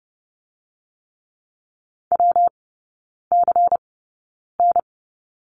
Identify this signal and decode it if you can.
Morse code, heard as a keyed tone, about 30 wpm, spaced as for 9 wpm, 700 Hz. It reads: WCN
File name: Lesson 19e - Letter w - Three Characters 30wpm